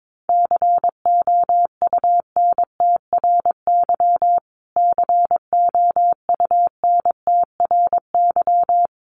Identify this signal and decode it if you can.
Morse code, heard as a keyed tone, about 22 wpm, 700 Hz. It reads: COUNTRY COUNTRY
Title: Top 500 Words Repeated 22wpm